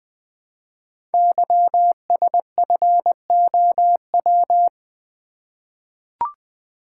Morse code, heard as a keyed tone, about 20 wpm, 700 Hz. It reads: YSFOW
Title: Code Groups - 5 Letter:Numbers - 20wpm